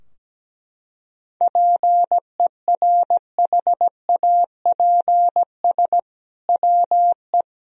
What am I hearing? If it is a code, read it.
Morse code, heard as a keyed tone, about 17 wpm, 700 Hz. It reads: PERHAPS WE
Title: Sentences from Top 1000 Words - Review Part 4 - 17wpm